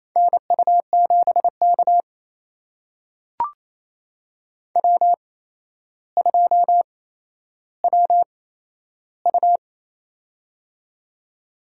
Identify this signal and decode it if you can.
Morse code, heard as a keyed tone, about 28 wpm, spaced as for 12 wpm, 700 Hz. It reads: NU7K W2WU
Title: USA Call Signs - 4 Characters - 8x Character Spacing - 28wpm